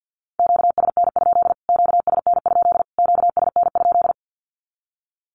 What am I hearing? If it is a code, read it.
Morse code, heard as a keyed tone, about 50 wpm, 700 Hz. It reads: QSN? QSN? QSN?
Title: Q-Codes Repeated 3x 50wpm